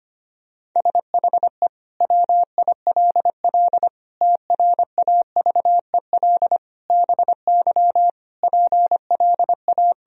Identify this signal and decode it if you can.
Morse code, heard as a keyed tone, about 25 wpm, 700 Hz. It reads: SHE WILL TRAVEL BY PLA
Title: Sentences from Top 1000 Words - Review Part 3 - 25wpm